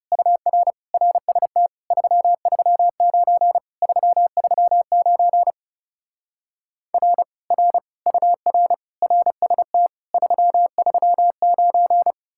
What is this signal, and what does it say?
Morse code, heard as a keyed tone, about 36 wpm, 700 Hz. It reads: UR RST 339 339 R R UR RST 339
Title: Speed-Racing Common QSO Phrases - 30wpm